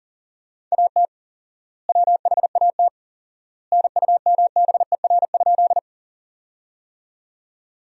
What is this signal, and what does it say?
Morse code, heard as a keyed tone, about 40 wpm, 700 Hz. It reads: AT WHAT NUMBER?
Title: 4x Word-Spacing - Sentences from Top 100 Words - 40wpm